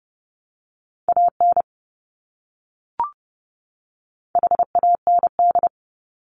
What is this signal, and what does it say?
Morse code, heard as a keyed tone, about 30 wpm, 700 Hz. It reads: AN HAND
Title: Top 200 Words 30wpm